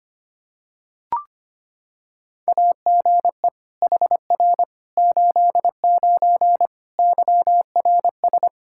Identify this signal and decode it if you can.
Morse code, heard as a keyed tone, about 25 wpm, 700 Hz. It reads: AGE HR 89 YRS
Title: Practice QSO - Exchange 3 - More Info - 25wpm